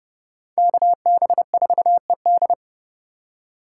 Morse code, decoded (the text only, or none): KB4ED